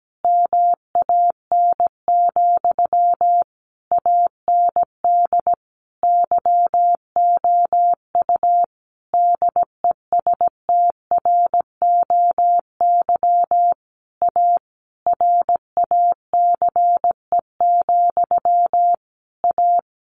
Morse code, heard as a keyed tone, about 17 wpm, 700 Hz. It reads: MAN, AND YOU DESTROY A RACE, A
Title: A Sound Of Thunder 17wpm